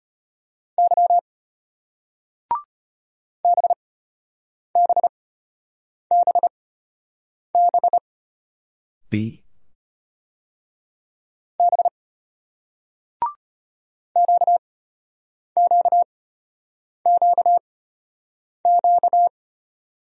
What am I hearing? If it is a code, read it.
Morse code, heard as a keyed tone, about 34 wpm, spaced as for 8 wpm, 700 Hz. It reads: Y BBBB B QQQQ